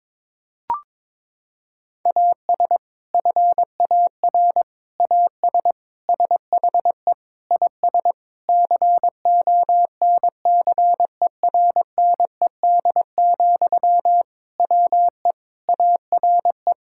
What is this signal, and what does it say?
Morse code, heard as a keyed tone, about 22 wpm, 700 Hz. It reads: AS FAR AS SHE IS CONCERNED, WE ARE
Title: Sentences from Top 1000 Words - Review Part 6 - 22wpm